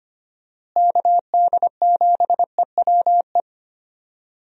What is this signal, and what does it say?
Morse code, heard as a keyed tone, about 25 wpm, 700 Hz. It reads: KD7EWE
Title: USA Call Signs - 6 Characters - Repeated 3x - 25wpm